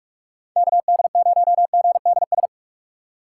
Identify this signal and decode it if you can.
Morse code, heard as a keyed tone, about 45 wpm, 700 Hz. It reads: KD0GDS
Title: USA Call Signs - 2x Character Spacing - 45wpm